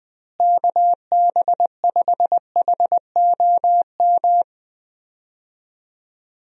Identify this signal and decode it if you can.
Morse code, heard as a keyed tone, about 20 wpm, 700 Hz. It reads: KB5HOM
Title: USA Call Signs - 6 Characters - Repeated 3x - 20wpm